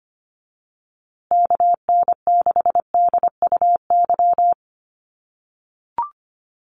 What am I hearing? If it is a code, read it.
Morse code, heard as a keyed tone, about 25 wpm, 700 Hz. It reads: KN6DUY
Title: USA Call Signs Repeated 3x 25wpm